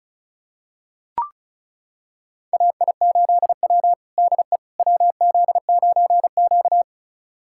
Morse code, heard as a keyed tone, about 36 wpm, 700 Hz. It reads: AI8W DE WZ9Q
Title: USA Callsign DE Callsign - 4 Characters - 35wpm